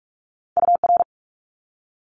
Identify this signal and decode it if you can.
Morse code, heard as a keyed tone, about 46 wpm, 700 Hz. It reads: UR
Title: Abbreviations Repeated 3x - 45wpm